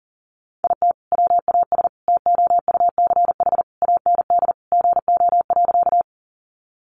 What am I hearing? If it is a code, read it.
Morse code, heard as a keyed tone, about 40 wpm, 700 Hz. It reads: IT WAS TOUCH AND GO.
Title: Sentences from Top 1200 Words 40wpm